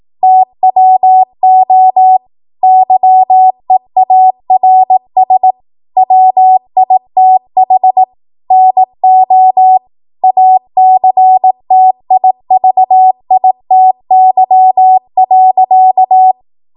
Morse code, heard as a keyed tone, about 18 wpm, 750 Hz. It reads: TWO YEARS WITH NO ACTIVITY.
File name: Archive / 18 WPM / 18WPM